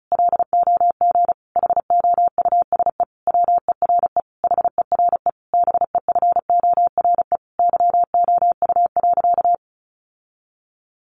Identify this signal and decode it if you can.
Morse code, heard as a keyed tone, about 34 wpm, 700 Hz. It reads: LOG HOUSE WERE HERE BEFORE YOU.
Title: Sentences from Top 1400 Words 35wpm